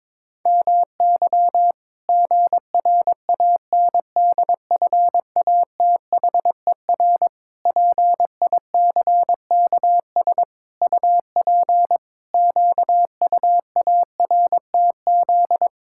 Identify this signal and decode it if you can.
Morse code, heard as a keyed tone, about 22 wpm, 700 Hz. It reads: MY GRANDFATHER PICKS UP QUARTZ
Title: Pangram Challenge v2 - 22wpm